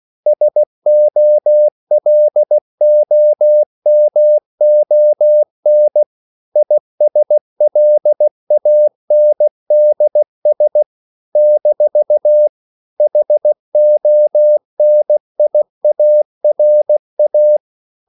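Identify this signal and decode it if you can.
Morse code, heard as a keyed tone, about 16 wpm, 600 Hz. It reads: SOLOMON ISLANDS - HONIARA
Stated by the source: CW practice of national capitals at 16 words per minute, from ON6ZQ.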